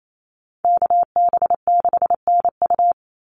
Morse code, heard as a keyed tone, about 28 wpm, 700 Hz. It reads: KB6NU
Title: Call Signs - 28wpm